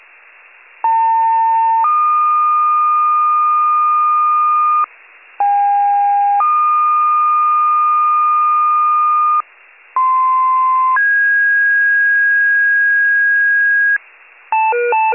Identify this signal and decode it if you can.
Two-tone page, heard as a one-second tone, then a three-second tone, about 900 then 1220 Hz, about 810 then 1220 Hz, about 1040 then 1650 Hz, with noise. It is heard from 0.8 to 14.0 s.